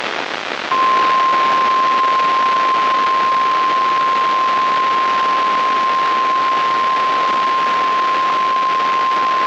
Emergency Alert System (SAME) tones (the 1050 Hz attention tone), with noise. It is heard from 0.7 to 8.3 s.
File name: eas messages / EAS WXM95 Towanda